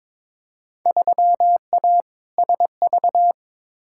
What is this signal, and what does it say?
Morse code, heard as a keyed tone, about 22 wpm, 700 Hz. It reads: 3A SV
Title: Field Day Exchanges - 22wpm